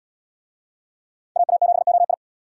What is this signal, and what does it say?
Morse code, heard as a keyed tone, about 74 wpm, 700 Hz. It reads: AA7OI